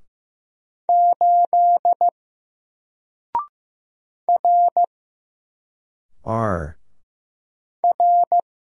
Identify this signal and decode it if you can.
Morse code, heard as a keyed tone, about 15 wpm, 700 Hz. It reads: 8 R R
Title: Single Letter-Number 15wpm